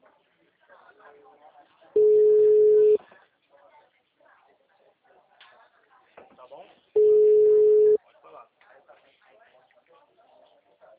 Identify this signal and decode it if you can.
Ringback tone (Europe, 425 Hz), with speech.